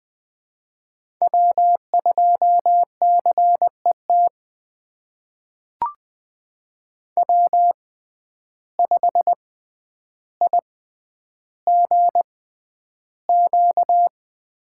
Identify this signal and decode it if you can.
Morse code, heard as a keyed tone, about 20 wpm, spaced as for 10 wpm, 700 Hz. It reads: W2CET W5IGQ
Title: USA Call Signs - 5 Characters - 6x Character Spacing - 20wpm